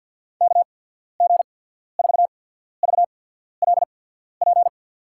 Morse code, heard as a keyed tone, about 54 wpm, spaced as for 14 wpm, 700 Hz. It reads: KG4VLP
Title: USA Call Signs - 6 Characters - 8x Character Spacing - 50wpm